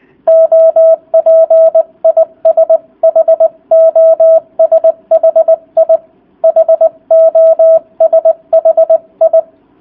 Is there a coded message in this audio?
Yes: Morse code from 0.3 to 9.4 s, heard as a keyed tone, about 20 wpm, 660 Hz, with radio noise. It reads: OP IS HOSHI HOSHI